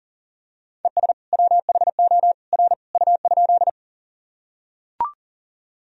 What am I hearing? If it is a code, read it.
Morse code, heard as a keyed tone, about 40 wpm, 700 Hz. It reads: ES WHO R U?
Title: Sentences from Top 100 Words and Abbreviations 40wpm